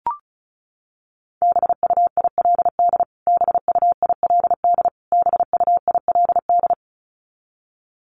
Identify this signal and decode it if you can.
Morse code, heard as a keyed tone, about 34 wpm, 700 Hz. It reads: BUILD BUILD BUILD
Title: Top 500 Words Repeated 35wpm